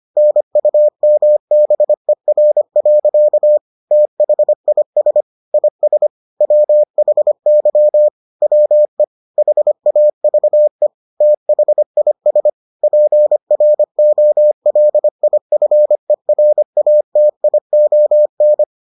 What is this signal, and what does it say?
Morse code, heard as a keyed tone, about 25 wpm, 600 Hz. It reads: NUMBER. THIS IS WHY WE HAVE THIS PROLIFERATION